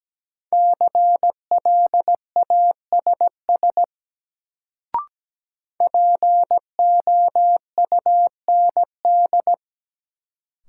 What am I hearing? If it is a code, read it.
Morse code, heard as a keyed tone, about 17 wpm, 700 Hz. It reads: CLASS POUND
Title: Sentences from Top 1000 Words - Review Part 2 - New Word Drill - 17wpm